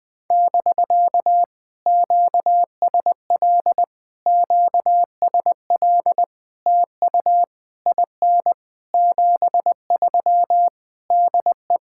Morse code, heard as a keyed tone, about 20 wpm, 700 Hz. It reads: <BK> QSL QSL TU IN 73 DE
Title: POTA QSO Practice - 20wpm